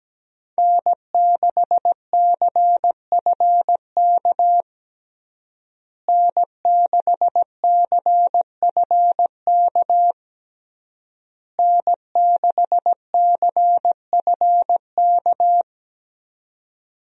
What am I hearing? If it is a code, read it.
Morse code, heard as a keyed tone, about 17 wpm, 700 Hz. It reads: N6CFK N6CFK N6CFK
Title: USA Call Signs - 5 Characters - Repeated 3x - 17wpm